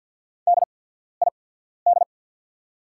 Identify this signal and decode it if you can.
Morse code, heard as a keyed tone, about 48 wpm, spaced as for 14 wpm, 700 Hz. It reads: DID
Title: 8x Character Spacing - Top 200 Words - 50wpm